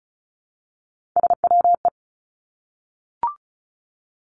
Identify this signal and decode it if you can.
Morse code, heard as a keyed tone, about 36 wpm, 700 Hz. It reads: SWE
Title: Three Challenging Letter-Number Combinations 35wpm